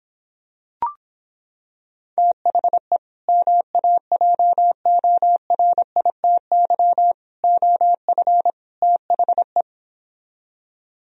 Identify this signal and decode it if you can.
Morse code, heard as a keyed tone, about 26 wpm, 700 Hz. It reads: THE MAJORITY OF THE